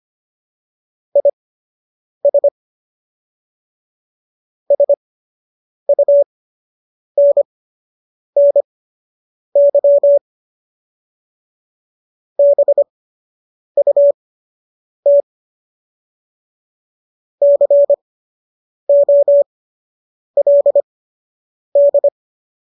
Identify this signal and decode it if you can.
Morse code, heard as a keyed tone, about 25 wpm, spaced as for 8 wpm, 570 Hz. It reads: IS SUNNY BUT COLD